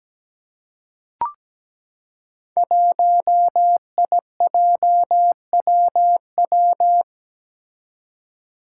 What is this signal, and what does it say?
Morse code, heard as a keyed tone, about 17 wpm, 700 Hz. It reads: 1IJWW
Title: Five Challenging Letter-Number Combinations 17wpm